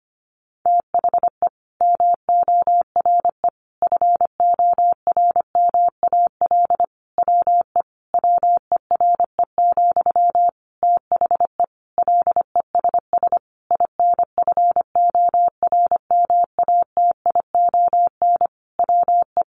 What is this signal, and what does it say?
Morse code, heard as a keyed tone, about 25 wpm, 700 Hz. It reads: THE MORE FORMAL WE WERE, THE LESS INFORMATION WE
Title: Sentences from Top 1300 Words 25wpm